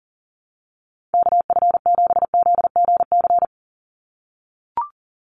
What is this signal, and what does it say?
Morse code, heard as a keyed tone, about 40 wpm, 700 Hz. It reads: KF7ZGC
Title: USA Call Signs 40wpm